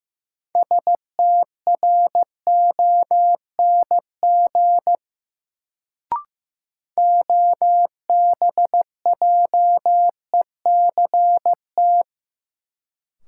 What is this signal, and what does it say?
Morse code, heard as a keyed tone, about 16 wpm, 700 Hz. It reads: STRONG OBJECT
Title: Sentences from Top 500 Words - New Word Drill - 15wpm